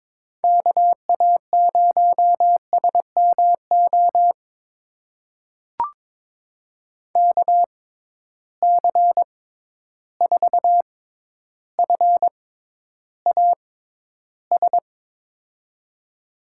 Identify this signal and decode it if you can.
Morse code, heard as a keyed tone, about 22 wpm, spaced as for 11 wpm, 700 Hz. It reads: KA0SMO KC4FAS
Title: USA Call Signs - 6 Characters - 6x Character Spacing - 22wpm